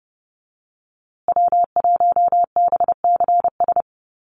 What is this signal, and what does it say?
Morse code, heard as a keyed tone, about 30 wpm, 700 Hz. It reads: W1BCS
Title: USA Call Signs Repeated 3x 30wpm